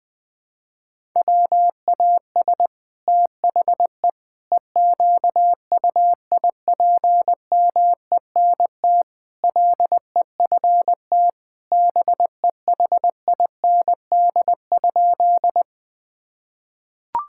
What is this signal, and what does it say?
Morse code, heard as a keyed tone, about 21 wpm, 700 Hz. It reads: WAS THE EQUIPMENT LEFT BEHIND?
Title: Sentences from Top 1000 Words - Review Part 5 - 20wpm